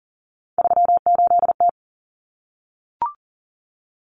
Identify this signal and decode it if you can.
Morse code, heard as a keyed tone, about 40 wpm, 700 Hz. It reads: 38T
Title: Common Cut Numbers - 3-Digit 40wpm